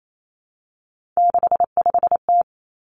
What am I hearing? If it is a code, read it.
Morse code, heard as a keyed tone, about 28 wpm, 700 Hz. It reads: 65T
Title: Three Challenging Letter-Number Combinations 28wpm